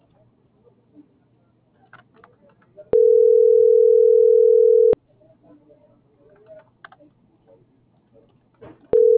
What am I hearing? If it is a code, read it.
Ringback tone (North America), 440 + 480 Hz.